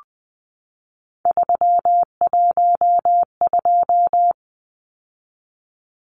Morse code, heard as a keyed tone, about 20 wpm, 700 Hz. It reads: 312
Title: Numbers - 3-Digit 20wpm